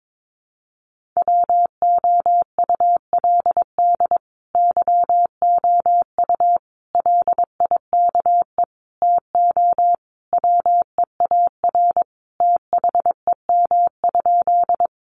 Morse code, heard as a keyed tone, about 22 wpm, 700 Hz. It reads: WOULD YOU LIKE TO WEAR THEM?
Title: Sentences from Top 1500 Words 22wpm